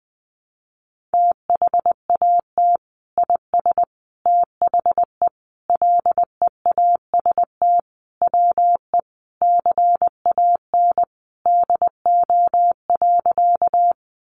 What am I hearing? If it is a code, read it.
Morse code, heard as a keyed tone, about 20 wpm, 700 Hz. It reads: THAT IS THE LEAST WE CAN DO.
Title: Sentences from Top 700 Words 20wpm